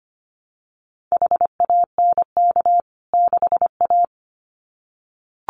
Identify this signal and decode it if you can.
Morse code, heard as a keyed tone, about 25 wpm, 700 Hz. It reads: HANK 6A
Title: CWT Exchange 25wpm